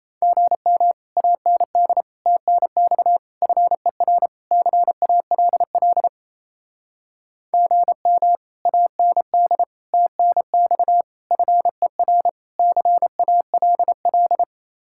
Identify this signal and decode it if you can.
Morse code, heard as a keyed tone, about 36 wpm, 700 Hz. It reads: GM AND TNX FER CALL GM AND TNX FER CALL